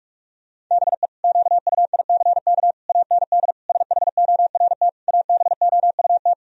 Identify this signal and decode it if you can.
Morse code, heard as a keyed tone, about 46 wpm, 700 Hz. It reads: BE QUICK AND SHORT ABOUT
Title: Sentences from Top 1000 Words - Review Part 2 - 45wpm